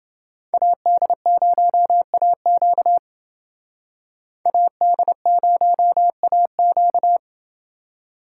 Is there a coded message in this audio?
Yes: Morse code from 0.5 to 7.2 s, heard as a keyed tone, about 28 wpm, 700 Hz. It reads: AD0AQ AD0AQ